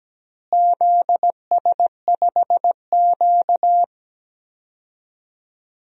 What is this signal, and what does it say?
Morse code, heard as a keyed tone, about 17 wpm, 700 Hz. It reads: ZS5Q
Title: DXCC-100 Call Signs - 4 Characters - Repeated 2x - 17wpm